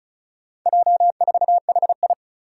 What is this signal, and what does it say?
Morse code, heard as a keyed tone, about 36 wpm, 700 Hz. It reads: J4HI
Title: Four Challenging Letter-Number Combinations 35wpm